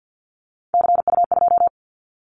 Morse code, heard as a keyed tone, about 50 wpm, 690 Hz. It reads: CU2
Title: Callsign Prefixes for Top 100 DXCC - 50wpm